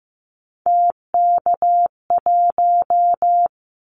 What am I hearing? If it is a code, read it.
Morse code, heard as a keyed tone, about 15 wpm, 700 Hz. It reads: TK1
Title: Callsign Prefixes for Top 100 DXCC - 15wpm